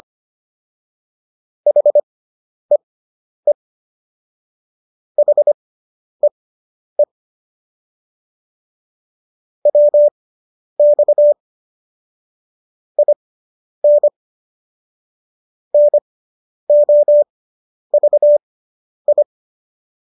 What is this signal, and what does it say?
Morse code, heard as a keyed tone, about 25 wpm, spaced as for 10 wpm, 590 Hz. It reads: HEE HEE WX IN NOVI